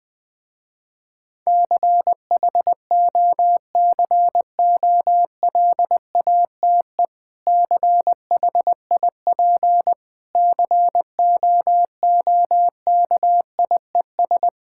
Morse code, heard as a keyed tone, about 20 wpm, 700 Hz. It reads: CHOCOLATE CHIP COOKIES
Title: Difficult Words with Hints - 20wpm